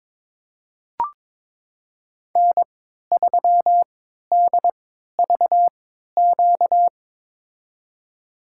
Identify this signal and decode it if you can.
Morse code, heard as a keyed tone, about 22 wpm, spaced as for 12 wpm, 700 Hz. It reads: N3DVQ
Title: USA Call Signs - 5 Characters - 3x Character Spacing - 22wpm